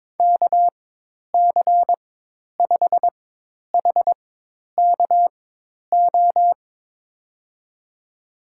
Morse code, heard as a keyed tone, about 22 wpm, spaced as for 10 wpm, 700 Hz. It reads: KC5HKO
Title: USA Call Signs - 4x Character Spacing - 22wpm